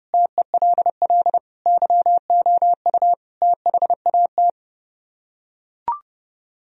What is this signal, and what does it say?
Morse code, heard as a keyed tone, about 31 wpm, 700 Hz. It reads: TELL YOU THAT